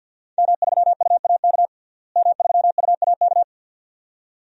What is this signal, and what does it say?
Morse code, heard as a keyed tone, about 53 wpm, 700 Hz. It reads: M3UAK M3UAK
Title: DXCC-100 Call Signs - 5 Characters - Repeated 2x - 50wpm